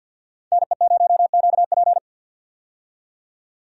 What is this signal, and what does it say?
Morse code, heard as a keyed tone, about 52 wpm, 700 Hz. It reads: NE0QP
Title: USA Call Signs - 5 Characters - 50wpm